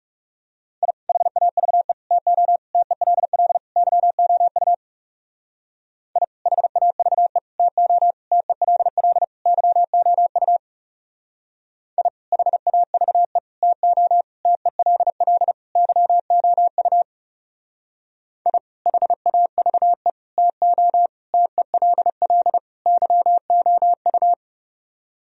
Morse code, heard as a keyed tone, about 38 wpm, 700 Hz. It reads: I HAVE TO TELL YOU I HAVE TO TELL YOU I HAVE TO TELL YOU I HAVE TO TELL YOU